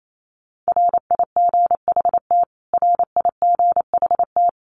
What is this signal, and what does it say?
Morse code, heard as a keyed tone, about 28 wpm, 700 Hz. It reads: RIGHT RIGHT
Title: Top 500 Words Repeated 28wpm